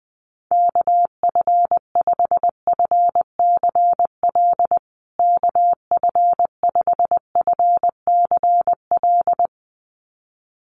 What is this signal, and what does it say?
Morse code, heard as a keyed tone, about 20 wpm, 700 Hz. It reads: KF5FCL KF5FCL
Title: USA Call Signs Repeated 2x 20wpm